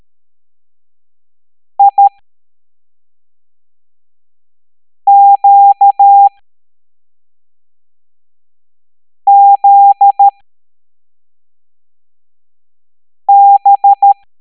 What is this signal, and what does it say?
Morse code, heard as a keyed tone, about 13 wpm, spaced as for 3 wpm, 800 Hz. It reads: IQZB
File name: hamdata / L07-EX02-HQP